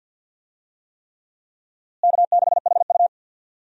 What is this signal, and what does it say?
Morse code, heard as a keyed tone, about 46 wpm, 700 Hz. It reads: K6HU
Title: USA Call Signs - 4 Characters - Repeated 3x - 50wpm